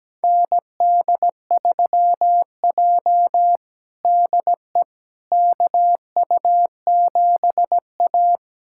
Morse code, heard as a keyed tone, about 17 wpm, 700 Hz. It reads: ND3J DE KU7A
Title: USA Callsign DE Callsign - 4 Characters - 17wpm